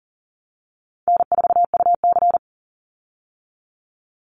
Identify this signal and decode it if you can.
Morse code, heard as a keyed tone, about 40 wpm, 700 Hz. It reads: N4UC
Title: Call Signs - 40wpm